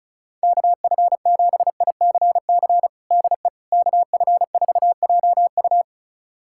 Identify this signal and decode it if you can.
Morse code, heard as a keyed tone, about 34 wpm, 700 Hz. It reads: KF7ICC DE KF4JU
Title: USA Callsign DE Callsign 35wpm